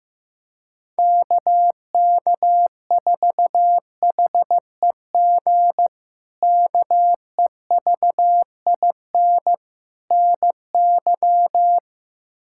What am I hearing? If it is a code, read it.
Morse code, heard as a keyed tone, about 15 wpm, 700 Hz. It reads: KK4HEG KEVIN NY
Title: K1USN SST QSO Practice - 15wpm